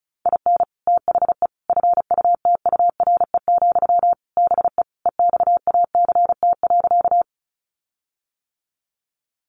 Morse code, heard as a keyed tone, about 34 wpm, 700 Hz. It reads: IN THE FUTURE, BE EXACT.
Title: Sentences from Top 1300 Words 35wpm